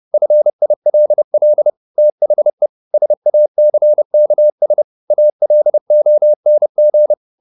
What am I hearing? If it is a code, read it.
Morse code, heard as a keyed tone, about 30 wpm, 600 Hz. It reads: FILL THE SACKS ALONG
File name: the old man and the sea / the old man and the sea 1 30wpm